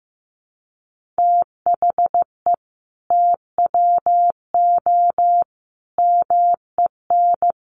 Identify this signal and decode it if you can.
Morse code, heard as a keyed tone, about 15 wpm, 700 Hz. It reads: THE TWO MEN
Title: Sentences from Top 1000 Words 15wpm